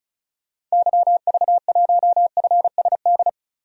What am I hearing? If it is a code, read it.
Morse code, heard as a keyed tone, about 34 wpm, 700 Hz. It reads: YV1FSD
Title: DXCC-100 Call Signs - 6 Characters - Repeated 2x - 35wpm